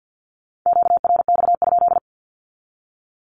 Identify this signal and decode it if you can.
Morse code, heard as a keyed tone, about 48 wpm, 700 Hz. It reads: QRX?
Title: Q-Codes 50wpm